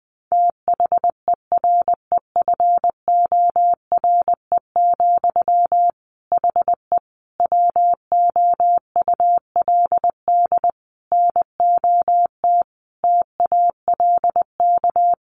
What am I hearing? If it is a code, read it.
Morse code, heard as a keyed tone, about 20 wpm, 700 Hz. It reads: THEREFORE, HE WOULD NOT TALK
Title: Sentences from Top 800 Words 20wpm